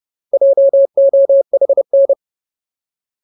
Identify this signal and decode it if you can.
Morse code, heard as a keyed tone, about 30 wpm, 550 Hz. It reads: JOHN
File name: Ultimate Word List / 30WPM Ultimate